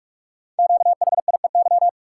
Morse code, heard as a keyed tone, about 44 wpm, 700 Hz. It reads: QHIEY
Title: Code Groups - 5 Letters - 45wpm